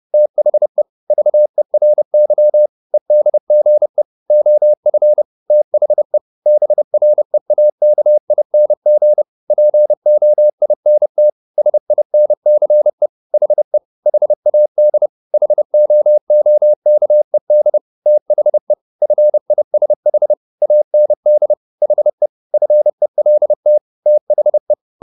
Morse code, heard as a keyed tone, about 30 wpm, 600 Hz. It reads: THE VERY EDGE OF THE BREAKING POINT SINCE HE HAD HOOKED THE FISH AND HE FELT THE
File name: the old man and the sea / the old man and the sea 1 30wpm